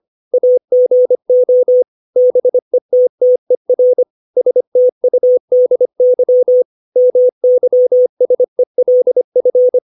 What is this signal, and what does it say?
Morse code, heard as a keyed tone, about 25 wpm, 500 Hz. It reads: AGO BETTER STUDY MYSELF
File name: 632 words / 632 Words 25WPM